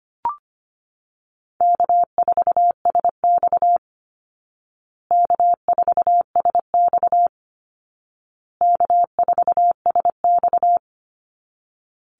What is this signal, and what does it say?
Morse code, heard as a keyed tone, about 25 wpm, 700 Hz. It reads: K4SX K4SX K4SX
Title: Easy Call Signs Repeated 3x - 25wpm